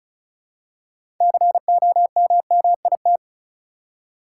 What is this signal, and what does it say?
Morse code, heard as a keyed tone, about 36 wpm, 700 Hz. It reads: COMMIT
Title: Six Letter Words - 35wpm